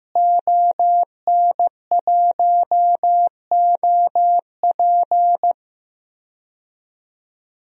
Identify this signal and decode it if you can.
Morse code, heard as a keyed tone, about 15 wpm, 700 Hz. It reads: ON1OP
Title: DXCC-100 Call Signs - 5 Characters - Repeated 2x - 15wpm